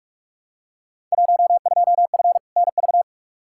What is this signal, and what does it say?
Morse code, heard as a keyed tone, about 46 wpm, 700 Hz. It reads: 12F NV
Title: Field Day Exchanges Encore - 45wpm